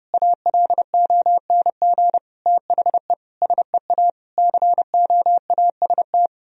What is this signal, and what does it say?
Morse code, heard as a keyed tone, about 30 wpm, 700 Hz. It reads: ALONG THE SEA COAST
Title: Sentences from Top 1000 Words - Review Part 4 - 30wpm